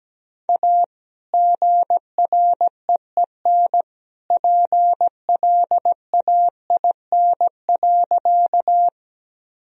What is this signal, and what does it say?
Morse code, heard as a keyed tone, about 17 wpm, 700 Hz. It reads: A GREEN PLAIN.
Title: Sentences from Top 1000 Words - Review Part 2 - 17wpm